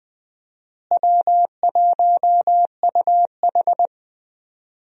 Morse code, heard as a keyed tone, about 20 wpm, 700 Hz. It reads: W1UH
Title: USA Call Signs - 4 Characters - Repeated 2x - 20wpm